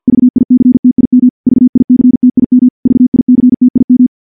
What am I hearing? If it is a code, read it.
Morse code, heard as a keyed tone, about 47 wpm, 250 Hz. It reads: VICTIM VICTIM VICTIM
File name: morse / morse-training-50WPM